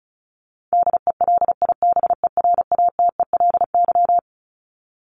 Morse code, heard as a keyed tone, about 34 wpm, 700 Hz. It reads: DELIBERATELY
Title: Top 1751-2000 Words 35wpm